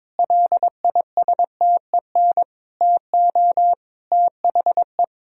Morse code, heard as a keyed tone, about 22 wpm, 700 Hz. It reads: LISTEN TO THE
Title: Sentences from Top 1000 Words - Review Part 3 - 22wpm